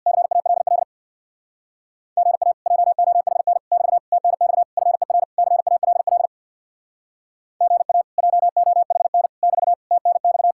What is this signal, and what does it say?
Morse code, heard as a keyed tone, about 63 wpm, 700 Hz. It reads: CALL GA JOHN <BT> TNX FER CALL GA JOHN <BT> TNX